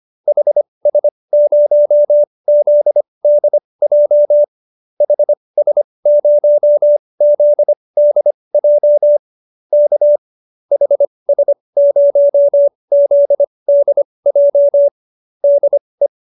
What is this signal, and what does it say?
Morse code, heard as a keyed tone, about 25 wpm, spaced as for 20 wpm, 590 Hz. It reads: HS0ZDJ HS0ZDJ K HS0ZDJ DE
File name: qso216b 20